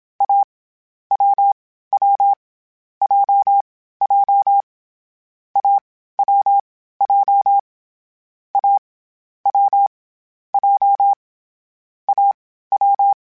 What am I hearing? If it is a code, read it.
Morse code, heard as a keyed tone, about 27 wpm, 800 Hz. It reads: A W W J J A W J A W J A W
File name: Difficult Combos / Difficult Combos-mp3s / AWJ @25